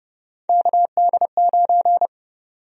Morse code, heard as a keyed tone, about 31 wpm, 700 Hz. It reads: KD9
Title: USA Call Sign Prefixes - 30wpm